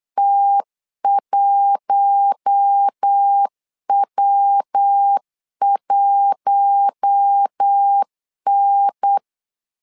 Morse code, heard as a keyed tone, about 8 wpm, 800 Hz. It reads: T1W1N
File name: LEZIONI / 4 Q1 8wpm 3035